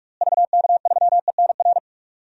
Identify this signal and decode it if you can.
Morse code, heard as a keyed tone, about 46 wpm, 700 Hz. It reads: VK3ENR
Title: DXCC-100 Call Signs - 6 Characters - Repeated 3x - 45wpm